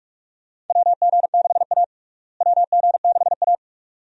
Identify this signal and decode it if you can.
Morse code, heard as a keyed tone, about 48 wpm, 700 Hz. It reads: WG6A WG6A
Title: USA Call Signs - 4 Characters - Repeated 2x - 45wpm